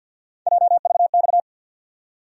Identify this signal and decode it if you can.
Morse code, heard as a keyed tone, about 52 wpm, 700 Hz. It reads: JVX
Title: USA Call Sign Postfixes - 3 Letters 50wpm